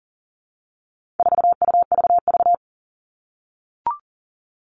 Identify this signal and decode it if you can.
Morse code, heard as a keyed tone, about 40 wpm, 700 Hz. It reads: 4UVV
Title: All Cut Numbers - 4-Digit 40wpm